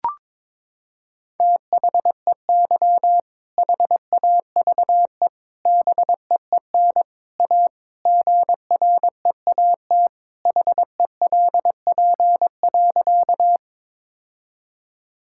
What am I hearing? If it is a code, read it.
Morse code, heard as a keyed tone, about 22 wpm, 700 Hz. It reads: THEY HAVE BEEN A GREAT HELP.
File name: practice / 22wpm / Sentences with 3-Word Phrases - 22wpm